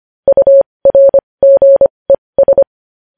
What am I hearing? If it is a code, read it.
Morse code, heard as a keyed tone, about 25 wpm, spaced as for 20 wpm, 550 Hz. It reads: URGES
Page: News headlines in Morse code at 20 WPM